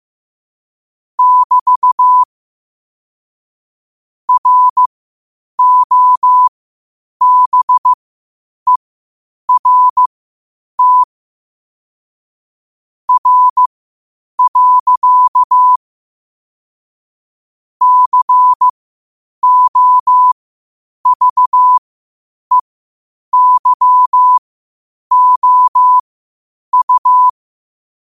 Morse code, heard as a keyed tone, about 15 wpm, spaced as for 8 wpm, 1000 Hz. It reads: <BT> ROBERT R. COVEYOU
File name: qotdfiles / 8wpm